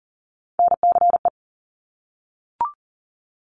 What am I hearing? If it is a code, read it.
Morse code, heard as a keyed tone, about 42 wpm, 700 Hz. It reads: NCE
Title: Top Three Letter Combinations in Words 40wpm